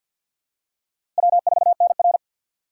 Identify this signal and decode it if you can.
Morse code, heard as a keyed tone, about 53 wpm, 700 Hz. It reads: W4NR
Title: USA Call Signs - 4 Characters - 3x Character Spacing - 50wpm